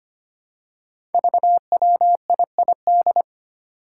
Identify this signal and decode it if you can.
Morse code, heard as a keyed tone, about 25 wpm, 700 Hz. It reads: VWIID
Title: Five Challenging Letter-Number Combinations 25wpm